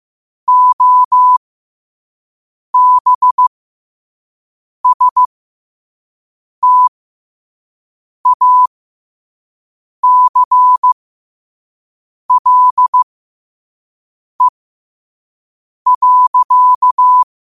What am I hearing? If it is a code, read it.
Morse code, heard as a keyed tone, about 15 wpm, spaced as for 5 wpm, 1000 Hz. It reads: OBSTACLE.